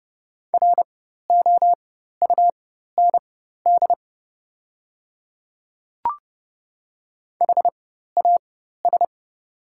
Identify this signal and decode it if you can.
Morse code, heard as a keyed tone, about 30 wpm, spaced as for 14 wpm, 700 Hz. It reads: ROUND HAS
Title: 4x Character Spacing - Top 300 Words - 30wpm